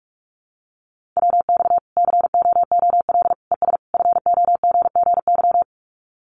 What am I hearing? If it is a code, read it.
Morse code, heard as a keyed tone, about 44 wpm, 700 Hz. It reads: WX COOL ES FOGGY
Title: Common QSO Phrases 45wpm